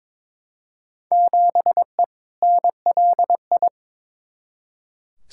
Morse code, heard as a keyed tone, about 21 wpm, 700 Hz. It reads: 7E NLI
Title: Field Day Exchanges Encore - 22wpm